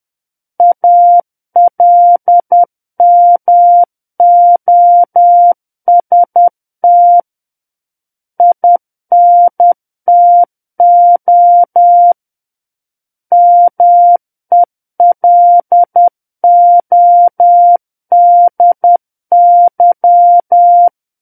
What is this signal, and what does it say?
Morse code, heard as a keyed tone, about 10 wpm, 700 Hz. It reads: ALMOST INTO MELODY
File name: Book One / 01-Chapter 1 - 10 WPM